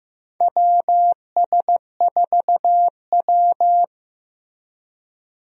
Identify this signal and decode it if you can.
Morse code, heard as a keyed tone, about 15 wpm, 700 Hz. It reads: WS4W
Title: USA Call Signs - 4 Characters - 15wpm